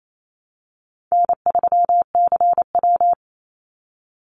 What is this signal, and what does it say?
Morse code, heard as a keyed tone, about 28 wpm, 700 Hz. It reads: N3CW
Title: Easy Call Signs - 28wpm